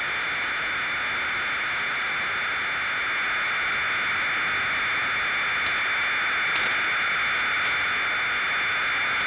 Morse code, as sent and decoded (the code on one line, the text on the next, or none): none